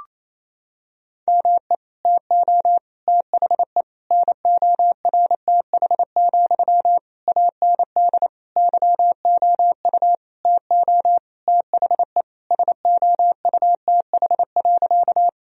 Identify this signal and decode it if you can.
Morse code, heard as a keyed tone, about 28 wpm, 700 Hz. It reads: ME TO THE NORTH, AND YOU TO THE SOUTH.
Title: Sentences from Top 1000 Words - Review Part 2 - 28wpm